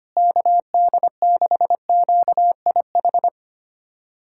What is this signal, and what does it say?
Morse code, heard as a keyed tone, about 25 wpm, 700 Hz. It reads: KD6QIH
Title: USA Callsign DE Callsign - 6 Characters - 25wpm